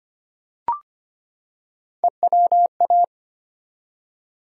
Morse code, heard as a keyed tone, about 25 wpm, 700 Hz. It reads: EWA
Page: morse-code-ninja-practice-25wpm